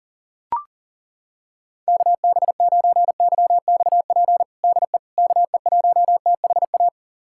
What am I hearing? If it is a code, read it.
Morse code, heard as a keyed tone, about 37 wpm, 700 Hz. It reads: KB9YXP DE KE1THA
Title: USA Callsign DE Callsign 40wpm